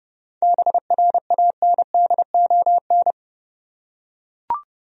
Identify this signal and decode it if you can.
Morse code, heard as a keyed tone, about 30 wpm, 700 Hz. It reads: BRANDON
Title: USA Common Names - 30wpm